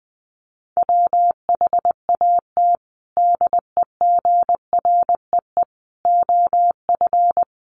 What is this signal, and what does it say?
Morse code, heard as a keyed tone, about 20 wpm, 700 Hz. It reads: WHAT DEGREE OF
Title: Sentences from Top 1500 Words 20wpm